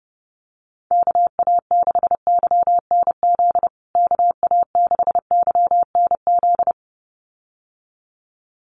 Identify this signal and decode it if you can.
Morse code, heard as a keyed tone, about 30 wpm, 700 Hz. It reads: KA6YNZ KA6YNZ
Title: USA Call Signs Repeated 2x 30wpm